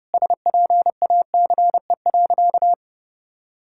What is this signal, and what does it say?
Morse code, heard as a keyed tone, about 29 wpm, 700 Hz. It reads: SPACE.
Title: Sentences from Top 1000 Words - Review Part 5 - 30wpm